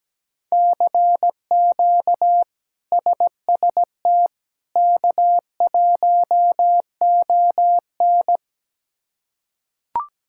Morse code, heard as a keyed tone, about 17 wpm, 700 Hz. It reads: CQ SST K1ON
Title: K1USN SST QSO Practice - 17wpm